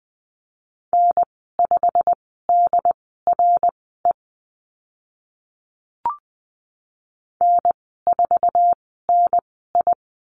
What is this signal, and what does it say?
Morse code, heard as a keyed tone, about 20 wpm, spaced as for 14 wpm, 700 Hz. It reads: N5DRE N4NI
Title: 088 Lesson 15g - Number 5 - Call Signs 20wpm